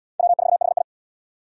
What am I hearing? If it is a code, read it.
Morse code, heard as a keyed tone, about 69 wpm, 700 Hz. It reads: W4HI